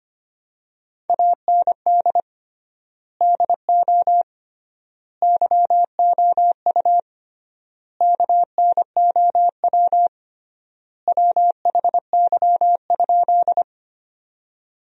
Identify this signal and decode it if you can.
Morse code, heard as a keyed tone, about 25 wpm, 700 Hz. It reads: AND DO YOU KNOW WHY?
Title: 3x Word-Spacing - Sentences from Top 200 Words - 25wpm